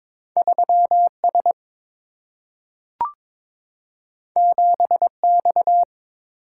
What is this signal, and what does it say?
Morse code, heard as a keyed tone, about 22 wpm, 700 Hz. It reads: 3 S 7 X
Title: Single-Letter Single-Number 22wpm